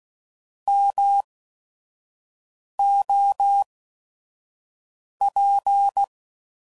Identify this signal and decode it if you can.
Morse code, heard as a keyed tone, about 16 wpm, spaced as for 5 wpm, 800 Hz. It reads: MOP